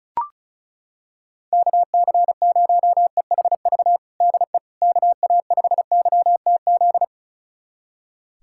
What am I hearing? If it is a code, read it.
Morse code, heard as a keyed tone, about 36 wpm, 700 Hz. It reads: KC0EHV DE KA5YTZ
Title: USA Callsign DE Callsign - 6 Characters - 35wpm